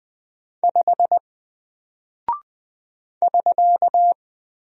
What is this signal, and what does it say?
Morse code, heard as a keyed tone, about 20 wpm, 700 Hz. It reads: 5 <SK>
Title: Single Character (Letter:Number:Punctuation:Prosign) 20wpm